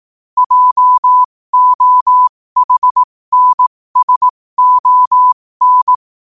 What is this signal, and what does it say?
Morse code, heard as a keyed tone, about 18 wpm, spaced as for 15 wpm, 1000 Hz. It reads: JOHNSON
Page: New quotes every day in morse code at 15 Words per minute.